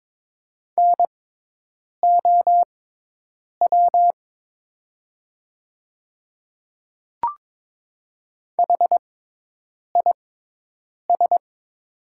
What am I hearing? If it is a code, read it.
Morse code, heard as a keyed tone, about 22 wpm, spaced as for 8 wpm, 700 Hz. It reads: NOW HIS
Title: 6x Character Spacing - Top 100 Words - 22wpm